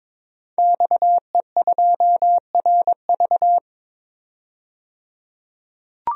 Morse code, heard as a keyed tone, about 22 wpm, 700 Hz. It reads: XE2RV
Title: DXCC-100 Call Signs - 5 Characters - 22wpm